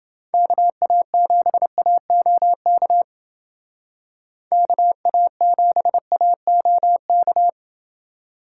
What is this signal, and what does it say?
Morse code, heard as a keyed tone, about 29 wpm, 700 Hz. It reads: KA7AOK KA7AOK